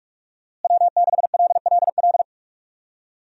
Morse code, heard as a keyed tone, about 43 wpm, 700 Hz. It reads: W6LLL
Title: USA Call Signs - 5 Characters - Repeated 3x - 45wpm